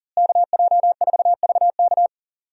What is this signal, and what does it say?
Morse code, heard as a keyed tone, about 40 wpm, 700 Hz. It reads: KJ4VK
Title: USA Call Signs - 5 Characters - Repeated 3x - 40wpm